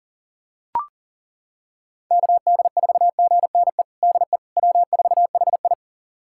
Morse code, heard as a keyed tone, about 40 wpm, 700 Hz. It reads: KD4GNE DE W4HI
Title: USA Callsign DE Callsign 40wpm